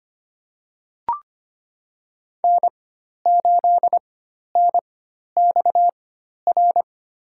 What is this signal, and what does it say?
Morse code, heard as a keyed tone, about 25 wpm, spaced as for 12 wpm, 700 Hz. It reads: N8NXR
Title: USA Call Signs - 5 Characters - 4x Character Spacing - 25wpm